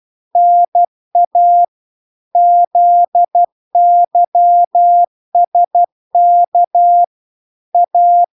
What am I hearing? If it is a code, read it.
Morse code, heard as a keyed tone, about 12 wpm, 700 Hz. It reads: NA ZYSK A